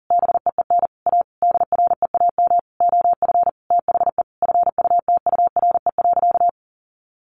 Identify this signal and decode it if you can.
Morse code, heard as a keyed tone, about 40 wpm, 700 Hz. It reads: BEEN A DREAM OF THE FUTURE.
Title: Sentences from Top 1500 Words 40wpm